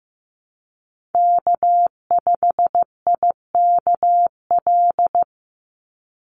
Morse code, heard as a keyed tone, about 15 wpm, 700 Hz. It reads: K5IKL
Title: USA Call Signs 15wpm